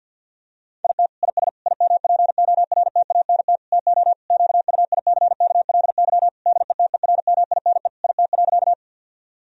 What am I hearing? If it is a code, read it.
Morse code, heard as a keyed tone, about 50 wpm, 700 Hz. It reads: IT IS IMPORTANT TO QUICKLY DETERMINE IT.
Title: Sentences from Top 1000 Words - Review Part 4 - 50wpm